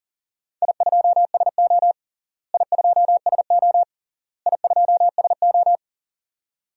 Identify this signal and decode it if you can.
Morse code, heard as a keyed tone, about 40 wpm, 700 Hz. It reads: I2SO I2SO I2SO
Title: DXCC-100 Call Signs - 4 Characters - Repeated 3x - 40wpm